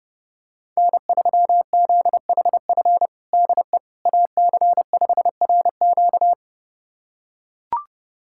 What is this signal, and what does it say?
Morse code, heard as a keyed tone, about 30 wpm, 700 Hz. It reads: N3ZHF DE AC5RQ
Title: USA Callsign DE Callsign - 5 Characters - 30wpm